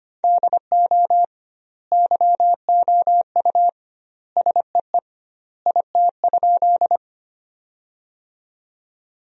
Morse code, heard as a keyed tone, about 25 wpm, 700 Hz. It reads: DO YOU SEE IT?
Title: 2x Word-Spacing - Sentences from Top 100 Words - 25wpm